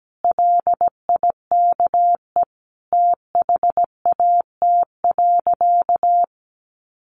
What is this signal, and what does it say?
Morse code, heard as a keyed tone, about 17 wpm, 700 Hz. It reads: LIKE THAT.
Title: Sentences from Top 1300 Words 17wpm